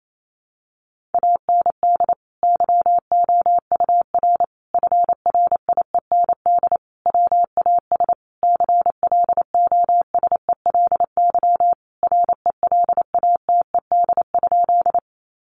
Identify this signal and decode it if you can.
Morse code, heard as a keyed tone, about 28 wpm, 700 Hz. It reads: AND YOUR FRIEND WAS CLOSELY RELATED?
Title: Sentences from Top 1400 Words 28wpm